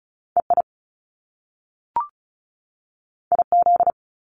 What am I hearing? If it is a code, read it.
Morse code, heard as a keyed tone, about 36 wpm, 700 Hz. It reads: EI IZ
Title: Top Two Letter Combinations in Words 35wpm